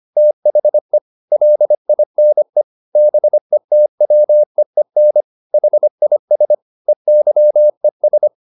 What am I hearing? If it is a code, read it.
Morse code, heard as a keyed tone, about 25 wpm, 600 Hz. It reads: THE LINE BETWEEN HIS EYES